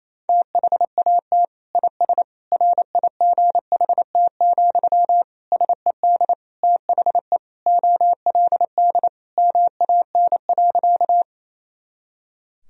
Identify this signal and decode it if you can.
Morse code, heard as a keyed tone, about 28 wpm, 700 Hz. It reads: THAT IS RIGHT, SED THE OLD MAN.
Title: Sentences from Top 200 Words and Abbreviations 28wpm